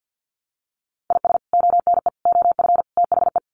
Morse code, heard as a keyed tone, about 50 wpm, 700 Hz. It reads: IS ONE OF THE
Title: Sets of 5 Words 50wpm